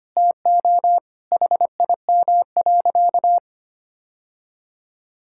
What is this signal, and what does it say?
Morse code, heard as a keyed tone, about 25 wpm, 700 Hz. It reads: TO HIM.
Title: Sentences from Top 1000 Words - Review Part 6 - 25wpm